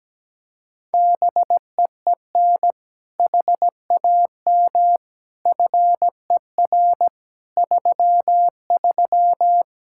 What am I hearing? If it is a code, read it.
Morse code, heard as a keyed tone, about 17 wpm, 700 Hz. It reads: BEEN HAM FER 33
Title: Practice QSO - Exchange 3 - More Info - 17wpm